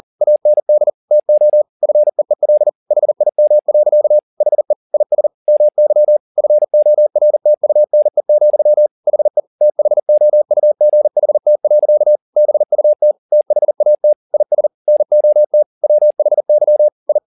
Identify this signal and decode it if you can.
Morse code, heard as a keyed tone, about 40 wpm, 600 Hz. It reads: AND TO FEEL HIM. HE IS MY FORTUNE, HE THOUGHT. BUT THAT IS NOT WHY I
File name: the old man and the sea 2 40wpm